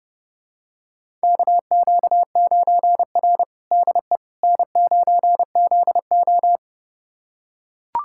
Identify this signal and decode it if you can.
Morse code, heard as a keyed tone, about 30 wpm, 700 Hz. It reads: KQ9R DE N9ZO
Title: USA Callsign DE Callsign - 4 Characters - 30wpm